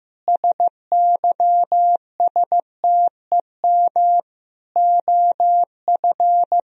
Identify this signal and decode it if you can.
Morse code, heard as a keyed tone, about 15 wpm, 700 Hz. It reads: SYSTEM OF
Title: Sentences from Top 1000 Words - Review Part 4 - 15wpm